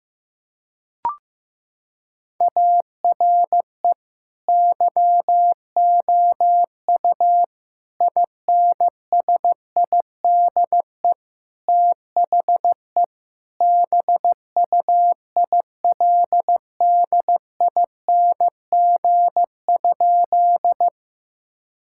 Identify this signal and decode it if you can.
Morse code, heard as a keyed tone, about 16 wpm, 700 Hz. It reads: ARE YOU INSIDE THE BUILDING?
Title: Sentences from Top 1000 Words - Review Part 6 - 15wpm